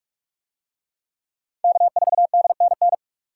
Morse code, heard as a keyed tone, about 46 wpm, 700 Hz. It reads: K4DNN